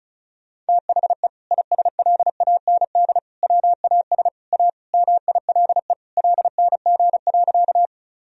Morse code, heard as a keyed tone, about 36 wpm, 700 Hz. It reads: THE ISLAND WAS A MILE LNG.
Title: Sentences from Top 500 Words and Abbreviations 35wpm